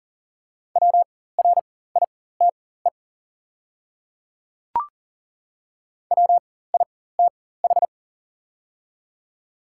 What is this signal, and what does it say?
Morse code, heard as a keyed tone, about 40 wpm, spaced as for 19 wpm, 700 Hz. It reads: WRITE WITH
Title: 4x Character Spacing - Top 100 Words - 40wpm